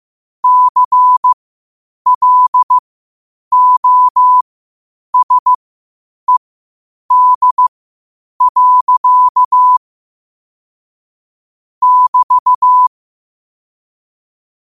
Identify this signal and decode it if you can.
Morse code, heard as a keyed tone, about 15 wpm, spaced as for 8 wpm, 1000 Hz. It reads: CLOSED. <BT>
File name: qotdfiles / 8wpm